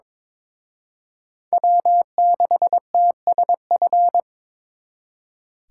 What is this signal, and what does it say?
Morse code, heard as a keyed tone, about 26 wpm, 700 Hz. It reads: W6TSF
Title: Speed Racing - USA Call Signs 22wpm